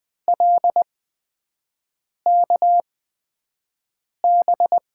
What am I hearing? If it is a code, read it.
Morse code, heard as a keyed tone, about 20 wpm, spaced as for 5 wpm, 700 Hz. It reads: LKB
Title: USA Call Signs - 6 Characters - 8x Character Spacing - 20wpm